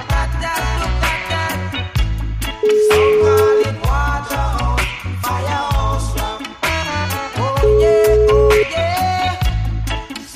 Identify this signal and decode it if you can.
Ringback tone (Europe, 425 Hz), with noise.